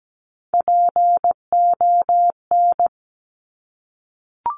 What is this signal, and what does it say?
Morse code, heard as a keyed tone, about 17 wpm, 700 Hz. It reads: PON
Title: Top Three Letter Combinations in Words 17wpm